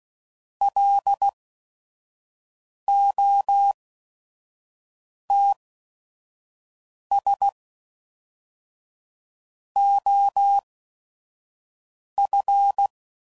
Morse code, heard as a keyed tone, about 16 wpm, spaced as for 5 wpm, 800 Hz. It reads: LOTS OF
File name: CW Code Groups @5WPM / Weather Sets / Weather 5 12